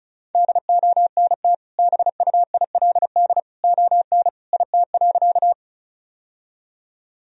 Morse code, heard as a keyed tone, about 36 wpm, 700 Hz. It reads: DONT BUILD ON IT.
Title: Sentences from Top 1000 Words - Review Part 1 - 35wpm